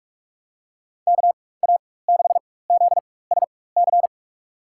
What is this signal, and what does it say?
Morse code, heard as a keyed tone, about 46 wpm, spaced as for 21 wpm, 700 Hz. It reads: KA6ZSC
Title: USA Call Signs - 6 Characters - 4x Character Spacing - 45wpm